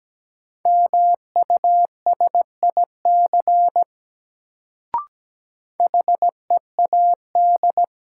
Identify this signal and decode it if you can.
Morse code, heard as a keyed tone, about 17 wpm, 700 Hz. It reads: MUSIC HEAD
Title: Sentences from Top 300 Words - New Word Drill - 17wpm